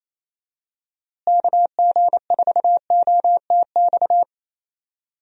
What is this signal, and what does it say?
Morse code, heard as a keyed tone, about 28 wpm, 700 Hz. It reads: KG4OTX
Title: USA Call Signs - 6 Characters - Repeated 3x - 28wpm